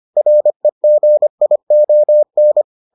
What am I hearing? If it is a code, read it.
Morse code, heard as a keyed tone, about 25 wpm, 600 Hz. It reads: REGION